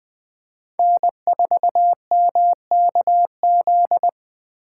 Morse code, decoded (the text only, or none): N4MKZ